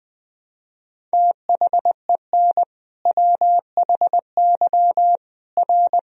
Morse code, heard as a keyed tone, about 21 wpm, 700 Hz. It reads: THEN WHY R
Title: Sentences from Top 200 Words and Abbreviations 20wpm